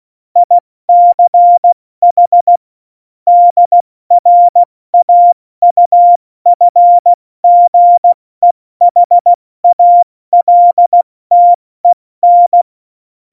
Morse code, heard as a keyed tone, about 16 wpm, 700 Hz. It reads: ICH DRAUFGEHALTEN
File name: Morsekurs / Witze / Witz 05. Januar 2021, 70 F80